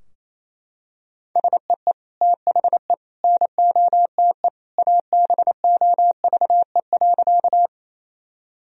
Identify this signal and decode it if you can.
Morse code, heard as a keyed tone, about 28 wpm, 700 Hz. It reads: SEE THE NOTE ABOVE.
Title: Sentences from Top 1000 Words - Review Part 2 - 28wpm